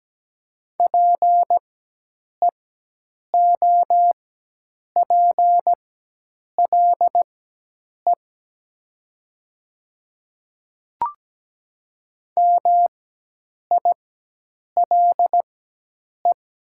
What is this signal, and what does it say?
Morse code, heard as a keyed tone, about 17 wpm, spaced as for 8 wpm, 700 Hz. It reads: PEOPLE MILE
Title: 4x Character Spacing - Top 300 Words - 17wpm